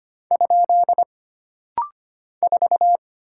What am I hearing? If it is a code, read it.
Morse code, heard as a keyed tone, about 25 wpm, 700 Hz. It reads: ? 4
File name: Single Character (Letter:Number:Punctuation:Prosign) - Rapid-Fire - 25wpm